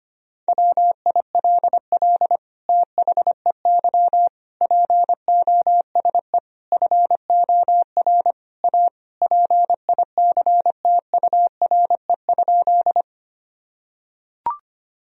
Morse code, heard as a keyed tone, about 25 wpm, 700 Hz. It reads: WILL THEY POSE FOR A PICTURE?
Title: Sentences from Top 1000 Words - Review Part 2 - 25wpm